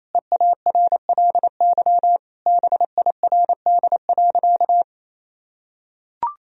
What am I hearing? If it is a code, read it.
Morse code, heard as a keyed tone, about 28 wpm, 700 Hz. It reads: EARLY BIRD.
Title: Sentences from Top 1000 Words - Review Part 2 - 28wpm